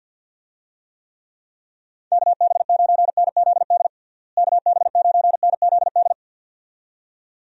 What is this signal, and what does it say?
Morse code, heard as a keyed tone, about 46 wpm, 700 Hz. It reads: KB9NZD KB9NZD
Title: USA Call Signs - 6 Characters - Repeated 2x - 50wpm